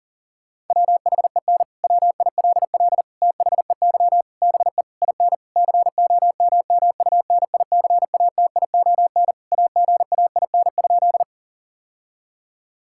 Morse code, heard as a keyed tone, about 40 wpm, 700 Hz. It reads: WHEN WILL THEY BE IN COMMUNICATION AGAIN?
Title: Sentences from Top 1000 Words - Review Part 6 - 40wpm